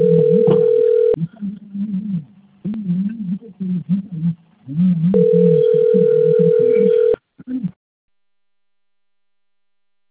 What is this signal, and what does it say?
Ringback tone (North America), 440 + 480 Hz.